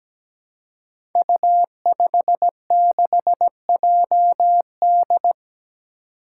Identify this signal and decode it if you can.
Morse code, heard as a keyed tone, about 17 wpm, 700 Hz. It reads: U56JD
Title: Five Challenging Letter-Number Combinations 17wpm